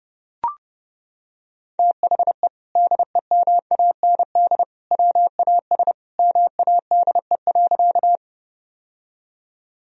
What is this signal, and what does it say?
Morse code, heard as a keyed tone, about 32 wpm, 700 Hz. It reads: THE DEMAND WAS MADE.
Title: Sentences from Top 1000 Words - Review Part 4 - 30wpm